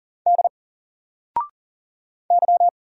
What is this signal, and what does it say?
Morse code, heard as a keyed tone, about 38 wpm, 700 Hz. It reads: D Y
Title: Single Letter-Number 40wpm